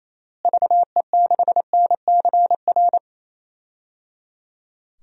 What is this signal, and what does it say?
Morse code, heard as a keyed tone, about 28 wpm, 700 Hz. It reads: VE6NCR
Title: DXCC-100 Call Signs - 6 Characters - Repeated 3x - 28wpm